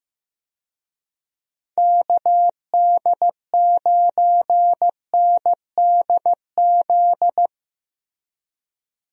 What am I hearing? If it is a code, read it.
Morse code, heard as a keyed tone, about 15 wpm, 700 Hz. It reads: KD9NDZ
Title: USA Call Signs - 6 Characters - Repeated 3x - 15wpm